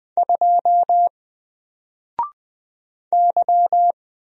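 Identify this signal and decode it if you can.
Morse code, heard as a keyed tone, about 20 wpm, 700 Hz. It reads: 2 Y
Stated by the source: Single letter and number Morse drill at 20 wpm, from N4LCD.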